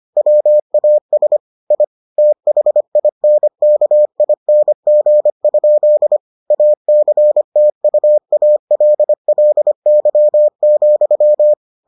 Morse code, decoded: WAS I THINKING? ACTUALLY,

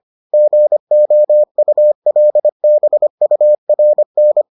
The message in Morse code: GOULBURN

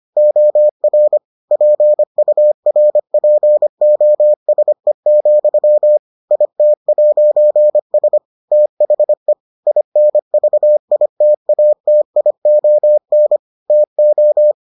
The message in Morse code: OR PURPOSE, IT'S THE INVITATION TO

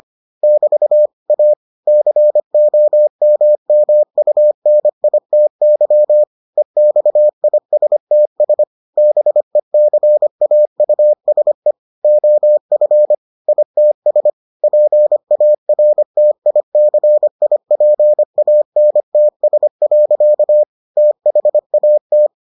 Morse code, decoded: <BT> A COMMUNITY EXISTS BECAUSE OF ITS PARTICIPANTS. THAT

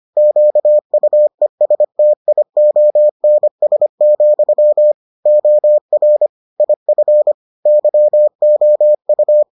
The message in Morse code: QUESTIONS, OR IF YOU

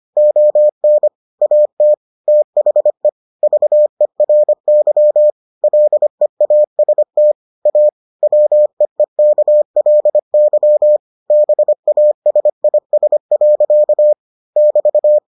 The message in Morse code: ON AT THE VERY LEAST A WEEKLY BASIS. <BT>

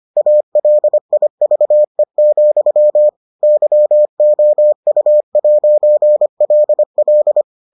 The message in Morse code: ALIVE, YOU'LL

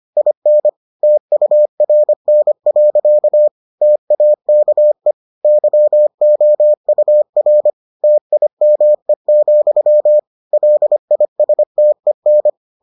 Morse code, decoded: IN TURN. TAKE YOUR TIME, LISTEN

